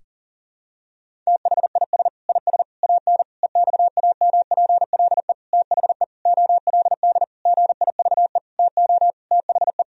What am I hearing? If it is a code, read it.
Morse code, heard as a keyed tone, about 37 wpm, 700 Hz. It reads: THIS IS AN EXAMPLE THE OLD GIVE TO THE
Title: Sentences from Top 1000 Words - Review Part 1 - 40wpm